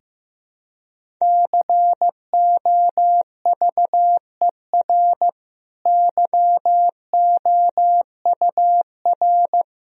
Morse code, heard as a keyed tone, about 15 wpm, 700 Hz. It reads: COVER YOUR
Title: Sentences from Top 1000 Words - Review Part 1 - 15wpm